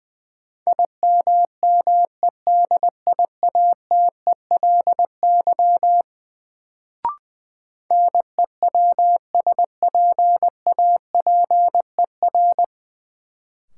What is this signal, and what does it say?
Morse code, heard as a keyed tone, about 20 wpm, 700 Hz. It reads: IMMEDIATELY NEWSPAPER
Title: Sentences from Top 1100 Words - New Word Drill Alt - 20wpm